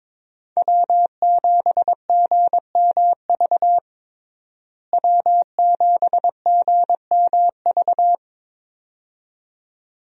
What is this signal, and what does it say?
Morse code, heard as a keyed tone, about 22 wpm, 700 Hz. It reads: W7GMV W7GMV
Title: USA Call Signs - 5 Characters - Repeated 2x - 22wpm